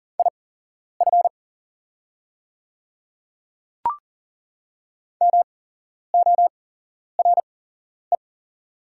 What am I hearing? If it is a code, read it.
Morse code, heard as a keyed tone, about 40 wpm, spaced as for 11 wpm, 700 Hz. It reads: IF MORE
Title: 8x Character Spacing - Top 500 Words - 40wpm